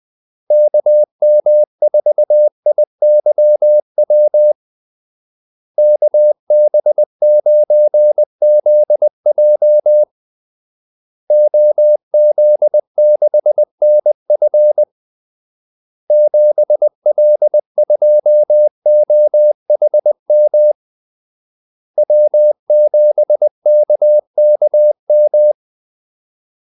Morse code, heard as a keyed tone, about 20 wpm, 600 Hz. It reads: KM4IYW KB9ZJ OZ6NF 7L2OHM W7KKM
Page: Kaldesignaler 20 wpm